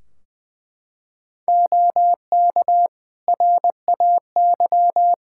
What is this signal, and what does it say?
Morse code, heard as a keyed tone, about 21 wpm, 700 Hz. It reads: OK RAY
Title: Practice QSO - Exchange 3 - More Info - 20wpm